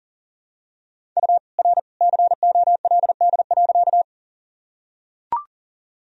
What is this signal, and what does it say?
Morse code, heard as a keyed tone, about 39 wpm, 700 Hz. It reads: U R COLD.
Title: Sentences from Top 400 Words and Abbreviations 40wpm